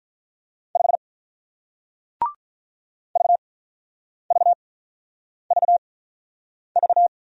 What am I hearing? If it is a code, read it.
Morse code, heard as a keyed tone, about 48 wpm, spaced as for 9 wpm, 700 Hz. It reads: 5 VVVV